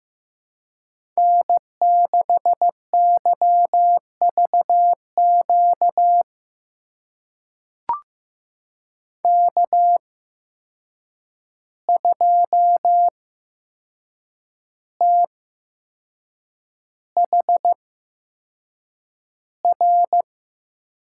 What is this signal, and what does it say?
Morse code, heard as a keyed tone, about 15 wpm, spaced as for 6 wpm, 700 Hz. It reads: N6YVQ K2THR
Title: USA Call Signs - 5 Characters - 8x Character Spacing - 15wpm